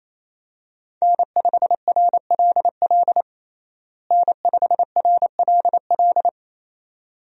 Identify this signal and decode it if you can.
Morse code, heard as a keyed tone, about 28 wpm, 700 Hz. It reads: N5RLL N5RLL
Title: USA Call Signs - 5 Characters - Repeated 2x - 28wpm